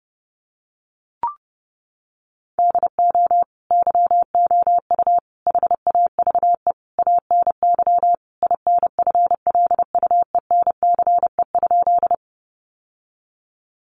Morse code, heard as a keyed tone, about 28 wpm, 700 Hz. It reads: DO YOU HAVE ANY INFLUENCE?
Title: Sentences from Top 1100 Words 30wpm